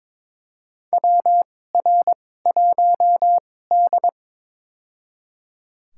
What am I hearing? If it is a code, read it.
Morse code, heard as a keyed tone, about 22 wpm, spaced as for 16 wpm, 700 Hz. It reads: WR1D
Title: USA Call Signs - 4 Characters - 2x Character Spacing - 22wpm